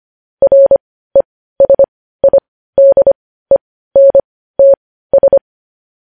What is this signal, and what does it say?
Morse code, heard as a keyed tone, about 25 wpm, spaced as for 15 wpm, 550 Hz. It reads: RESIDENTS
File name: podcast / News headlines-1x-25x15